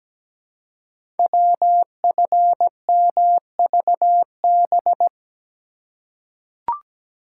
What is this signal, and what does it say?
Morse code, heard as a keyed tone, about 17 wpm, 700 Hz. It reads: WFMVB
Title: Code Groups - 5 Letters - 17wpm